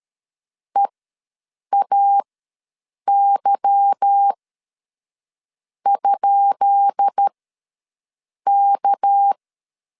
Morse code, heard as a keyed tone, about 13 wpm, spaced as for 6 wpm, 800 Hz. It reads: EAY?K
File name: LEZIONI / 10 SIGNS / 10B INT ws 12wpm 3527